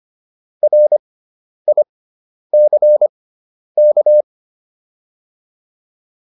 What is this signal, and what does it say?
Morse code, heard as a keyed tone, about 25 wpm, spaced as for 10 wpm, 600 Hz. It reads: RICK